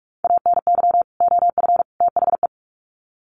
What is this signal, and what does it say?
Morse code, heard as a keyed tone, about 44 wpm, 700 Hz. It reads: ANY OF THE
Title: Sets of 3 Words 45wpm